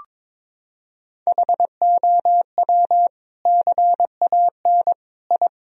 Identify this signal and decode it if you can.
Morse code, heard as a keyed tone, about 23 wpm, 700 Hz. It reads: HOW CAN I
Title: Sentences from Top 1000 Words - Review Part 2 - 22wpm